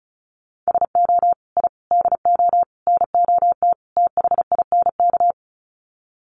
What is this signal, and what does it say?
Morse code, heard as a keyed tone, about 34 wpm, 700 Hz. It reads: SO I DO NOT THINK
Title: Sets of 5 Words 35wpm